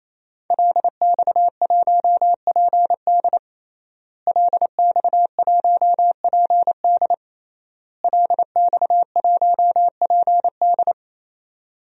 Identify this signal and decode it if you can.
Morse code, heard as a keyed tone, about 28 wpm, 700 Hz. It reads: LX1PD LX1PD LX1PD
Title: DXCC-100 Call Signs - 5 Characters - Repeated 3x - 28wpm